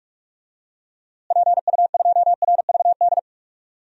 Morse code, heard as a keyed tone, about 44 wpm, 700 Hz. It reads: PU2RVD
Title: DXCC-100 Call Signs - 6 Characters - Repeated 2x - 45wpm